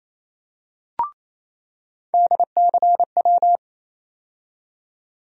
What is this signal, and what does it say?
Morse code, heard as a keyed tone, about 28 wpm, 700 Hz. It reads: DCW
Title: USA Call Sign Postfixes - 3 Letters 28wpm